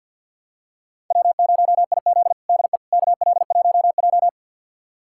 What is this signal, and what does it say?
Morse code, heard as a keyed tone, about 51 wpm, 700 Hz. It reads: W0IZ DE KL1J
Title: USA Callsign DE Callsign - 4 Characters - 50wpm